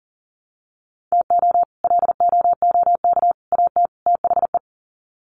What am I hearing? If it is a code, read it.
Morse code, heard as a keyed tone, about 40 wpm, 700 Hz. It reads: TO LOOK AT THE
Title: Sets of 4 Words 40wpm